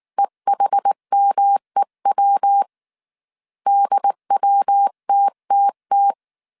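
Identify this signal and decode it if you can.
Morse code, heard as a keyed tone, about 19 wpm, 800 Hz. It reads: EHMEW DWTTT
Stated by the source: Morse code dictation at 18 wpm from the IW3SGT QRS Morse code training course, five-character groups of 300 characters.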